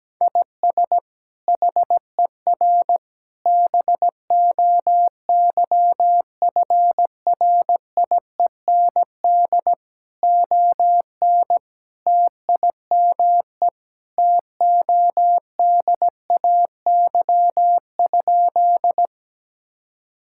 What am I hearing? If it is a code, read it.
Morse code, heard as a keyed tone, about 17 wpm, 700 Hz. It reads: IS HER BOYFRIEND ON TIME TODAY?
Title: Sentences from Top 1000 Words - Review Part 5 - 17wpm